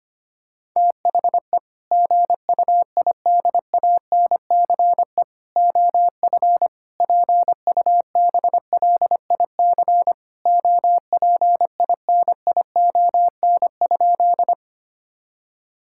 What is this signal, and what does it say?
Morse code, heard as a keyed tone, about 25 wpm, 700 Hz. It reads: THE GUIDANCE OF PUBLIC OPINION?
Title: Sentences from Top 1600 Words 25wpm